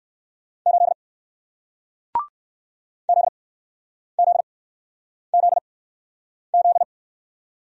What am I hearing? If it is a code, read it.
Morse code, heard as a keyed tone, about 60 wpm, spaced as for 9 wpm, 700 Hz. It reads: 8 ZZZZ